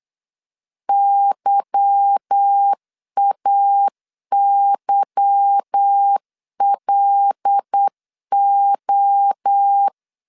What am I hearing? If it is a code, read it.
Morse code, heard as a keyed tone, about 8 wpm, 800 Hz. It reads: YAYLO
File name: LEZIONI / 8 YJ45 / 8 Y 8wpm 3245